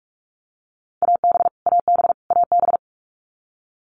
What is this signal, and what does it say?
Morse code, heard as a keyed tone, about 44 wpm, 700 Hz. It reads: AB AB AB
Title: Canadian Provinces and Territories - Repeated 3x - 45wpm